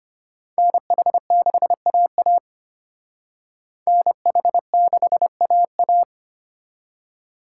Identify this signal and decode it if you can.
Morse code, heard as a keyed tone, about 32 wpm, 700 Hz. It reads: NH6AA NH6AA